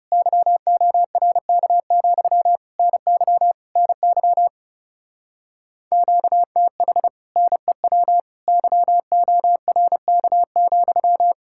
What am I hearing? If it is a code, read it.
Morse code, heard as a keyed tone, about 36 wpm, 700 Hz. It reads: YORK, NY NY QTH NEW YORK,
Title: Speed-Racing Common QSO Phrases - 30wpm